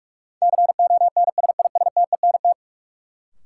Morse code, heard as a keyed tone, about 46 wpm, 700 Hz. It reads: CONSISTENT
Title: Sentences from Top 1500 Words - New Word Drill - 45wpm